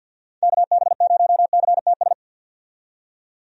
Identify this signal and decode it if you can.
Morse code, heard as a keyed tone, about 47 wpm, 700 Hz. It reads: KB0CTS
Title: USA Call Signs - 6 Characters - 50wpm